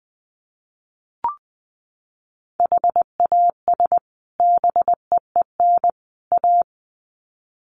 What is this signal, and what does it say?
Morse code, heard as a keyed tone, about 20 wpm, 700 Hz. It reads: HAS BEEN A
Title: Sets of 3 Words 20wpm